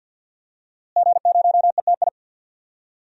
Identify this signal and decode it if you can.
Morse code, heard as a keyed tone, about 51 wpm, 700 Hz. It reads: G0ETI
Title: DXCC-100 Call Signs - 5 Characters - Repeated 3x - 50wpm